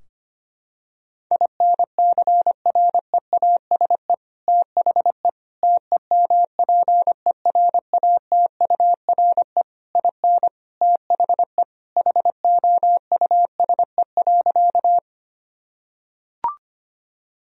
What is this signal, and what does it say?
Morse code, heard as a keyed tone, about 25 wpm, 700 Hz. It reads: INCREASE THE TEMPERATURE IN THE HOUSE.
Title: Sentences from Top 1000 Words - Review Part 4 - 25wpm